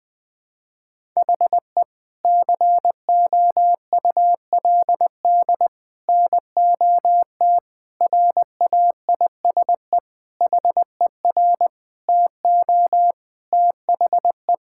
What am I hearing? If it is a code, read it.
Morse code, heard as a keyed tone, about 20 wpm, 700 Hz. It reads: HE COULD NOT RAISE HER TO THE
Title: Sentences from Top 1600 Words 20wpm